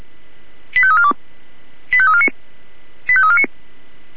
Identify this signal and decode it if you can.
Five-tone selective call (ZVEI), heard as quick runs of five tones, tones of about 70 ms. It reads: 85342 85349 85349